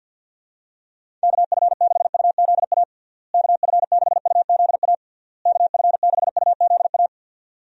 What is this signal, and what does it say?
Morse code, heard as a keyed tone, about 52 wpm, 700 Hz. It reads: KF6UZA KF6UZA KF6UZA
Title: USA Call Signs - 6 Characters - Repeated 3x - 50wpm